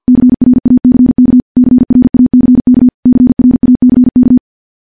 Morse code, heard as a keyed tone, about 50 wpm, 250 Hz. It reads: CRACK CRACK CRACK
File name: morse / morse-training-50WPM